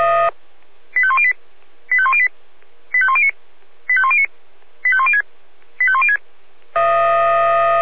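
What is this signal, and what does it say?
Five-tone selective call (ZVEI), heard as quick runs of five tones, tones of about 70 ms. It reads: 85108 85108 85109 85109 85106 85106
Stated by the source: ZVEI five-tone call 85102 received on SDR.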